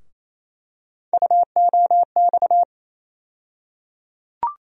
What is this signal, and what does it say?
Morse code, heard as a keyed tone, about 28 wpm, 700 Hz. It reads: UOX
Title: USA Call Sign Postfixes - 3 Letters 28wpm